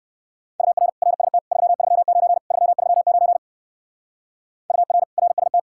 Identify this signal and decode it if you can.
Morse code, heard as a keyed tone, about 59 wpm, 700 Hz. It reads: UR RST 339 339 UR RST